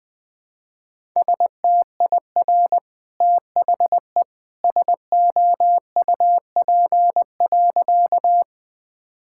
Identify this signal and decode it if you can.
Morse code, heard as a keyed tone, about 20 wpm, 700 Hz. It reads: STIR THE SOUP.
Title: Easy Sentences with Hints - 20wpm